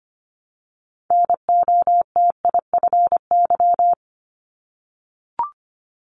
Morse code, heard as a keyed tone, about 25 wpm, 700 Hz. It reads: NOTIFY
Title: Multi-letter Phonemes 25wpm